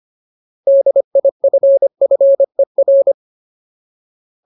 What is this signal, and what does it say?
Morse code, heard as a keyed tone, about 25 wpm, 550 Hz. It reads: DIFFER
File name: Ultimate Word List / 25WPM Ultimate